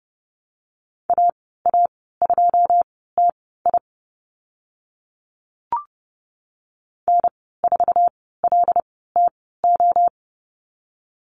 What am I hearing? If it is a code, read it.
Morse code, heard as a keyed tone, about 30 wpm, spaced as for 17 wpm, 700 Hz. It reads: AA2TI N4LTO
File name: Lesson 14g - Number 2 - Call Signs 30wpm